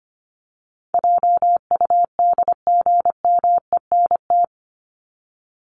Morse code, heard as a keyed tone, about 25 wpm, 700 Hz. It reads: JUDGMENT
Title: Top 1751-2000 Words 25wpm